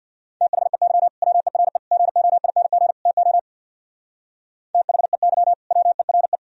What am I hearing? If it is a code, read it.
Morse code, heard as a keyed tone, about 55 wpm, 700 Hz. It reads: THEY WERE GOING TO THEY WERE
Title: Speed-Racing - Sets of 4 Words 50wpm